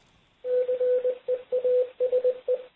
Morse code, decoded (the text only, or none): CEASE